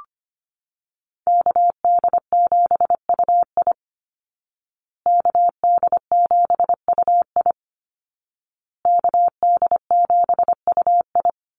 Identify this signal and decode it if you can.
Morse code, heard as a keyed tone, about 25 wpm, 700 Hz. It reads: KD7UI KD7UI KD7UI
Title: Call Signs Repeated 2x - 25wpm